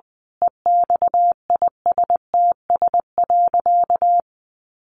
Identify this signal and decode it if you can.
Morse code, heard as a keyed tone, about 20 wpm, 700 Hz. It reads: EXISTS.
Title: A Sound Of Thunder 20wpm